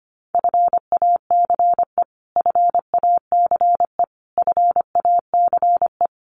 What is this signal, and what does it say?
Morse code, heard as a keyed tone, about 25 wpm, 700 Hz. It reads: FACE FACE FACE
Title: Top 300 Words Repeated 25wpm